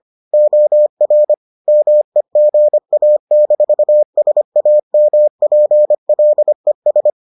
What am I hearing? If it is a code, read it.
Morse code, heard as a keyed tone, about 25 wpm, 600 Hz. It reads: OR MEGA-SAMPLES